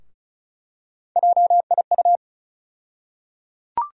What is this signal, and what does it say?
Morse code, heard as a keyed tone, about 36 wpm, 700 Hz. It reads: JIU
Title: USA Call Sign Postfixes - 3 Letters 35wpm